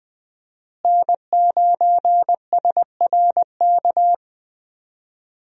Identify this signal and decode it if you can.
Morse code, heard as a keyed tone, about 20 wpm, 700 Hz. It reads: N9SRK
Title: USA Call Signs - 5 Characters - Repeated 2x - 20wpm